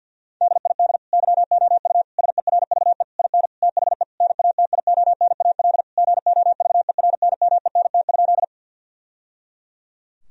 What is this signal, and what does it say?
Morse code, heard as a keyed tone, about 50 wpm, 700 Hz. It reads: DID YOU SERVE IN THE NATIONAL GOVERNMENT?
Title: Sentences from Top 1000 Words - Review Part 4 - 50wpm